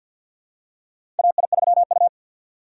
Morse code, heard as a keyed tone, about 47 wpm, 700 Hz. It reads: AI3U
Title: USA Call Signs - 4 Characters - 3x Character Spacing - 50wpm